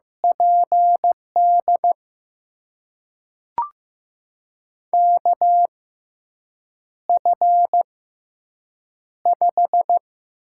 Morse code, heard as a keyed tone, about 15 wpm, spaced as for 7 wpm, 700 Hz. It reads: PD KF5
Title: USA Call Signs - 6 Characters - 6x Character Spacing - 15wpm